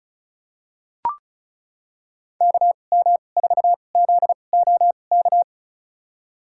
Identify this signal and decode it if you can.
Morse code, heard as a keyed tone, about 36 wpm, spaced as for 25 wpm, 700 Hz. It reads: KM4ZOK
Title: USA Call Signs - 6 Characters - 2x Character Spacing - 35wpm